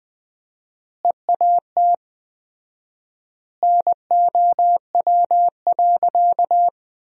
Morse code, heard as a keyed tone, about 20 wpm, 700 Hz. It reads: EAT NOW.
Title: 4x Word-Spacing - Sentences from Top 300 Words - 20wpm